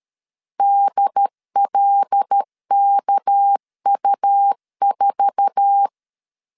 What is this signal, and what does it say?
Morse code, heard as a keyed tone, about 13 wpm, 800 Hz. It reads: DLKU4
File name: LEZIONI / 10 SIGNS / 10D COMA 12wpm 2544